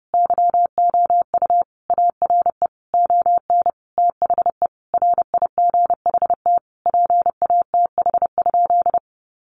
Morse code, heard as a keyed tone, about 30 wpm, 700 Hz. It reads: YOU ARE ON THE RIGHT PATH?
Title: Sentences from Top 1400 Words 30wpm